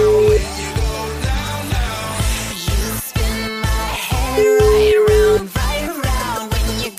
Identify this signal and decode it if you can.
Ringback tone (Europe, 425 Hz), with noise.